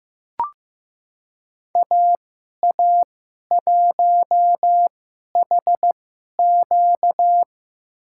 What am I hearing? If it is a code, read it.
Morse code, heard as a keyed tone, about 15 wpm, spaced as for 11 wpm, 700 Hz. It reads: AA1HQ
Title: USA Call Signs - 5 Characters - 2x Character Spacing - 15wpm